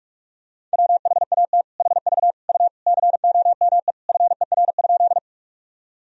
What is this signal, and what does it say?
Morse code, heard as a keyed tone, about 46 wpm, 700 Hz. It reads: WHAT HV U COME FER?
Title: Sentences from Top 100 Words and Abbreviations 45wpm